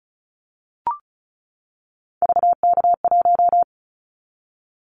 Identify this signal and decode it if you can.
Morse code, heard as a keyed tone, about 34 wpm, 700 Hz. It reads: VK1
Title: Callsign Prefixes for Top 50 DXCC - 35wpm